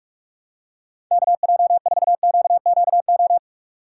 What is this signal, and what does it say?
Morse code, heard as a keyed tone, about 44 wpm, 700 Hz. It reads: KJ4QQO
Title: USA Call Signs - 6 Characters - 3x Character Spacing - 45wpm